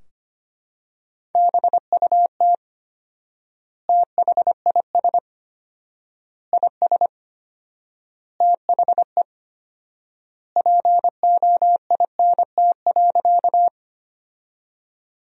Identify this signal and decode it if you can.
Morse code, heard as a keyed tone, about 25 wpm, 700 Hz. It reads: BUT THIS IS THE POINT.
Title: 4x Word-Spacing - Sentences from Top 200 Words - 25wpm